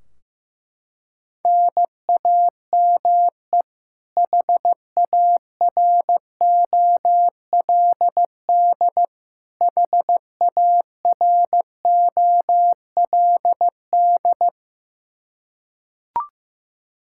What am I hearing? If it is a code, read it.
Morse code, heard as a keyed tone, about 15 wpm, 700 Hz. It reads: NAME HAROLD HAROLD
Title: K1USN SST QSO Practice 15wpm